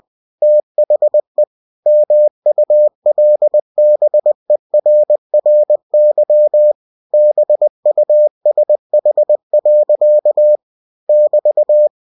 Morse code, heard as a keyed tone, about 20 wpm, 600 Hz. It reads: THE MULBERRY BUSH. <BT>
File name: cw-books / winnie-the-pooh / winnie-20wpm-page-0045